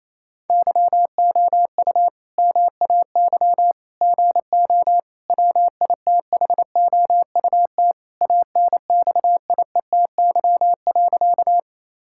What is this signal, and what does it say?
Morse code, heard as a keyed tone, about 28 wpm, 700 Hz. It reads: YOU MAY GO WITHOUT ANXIETY.
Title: Sentences from Top 1600 Words 28wpm